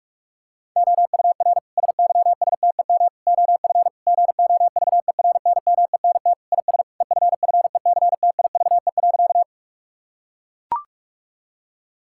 Morse code, heard as a keyed tone, about 42 wpm, 700 Hz. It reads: OUR SYSTEM OF GOVERNMENT IS EFFECTIVE.
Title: Sentences from Top 1000 Words - Review Part 6 - 45wpm